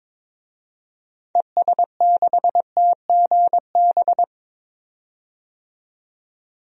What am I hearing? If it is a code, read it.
Morse code, heard as a keyed tone, about 21 wpm, 700 Hz. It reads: ES6TGB
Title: DXCC-100 Call Signs - 6 Characters - 22wpm